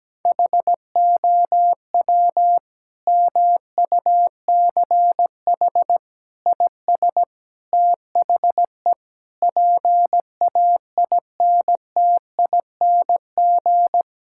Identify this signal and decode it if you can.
Morse code, heard as a keyed tone, about 17 wpm, 700 Hz. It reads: HOW MUCH IS THE PAINTING
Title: Sentences from Top 1000 Words - Review Part 6 - 17wpm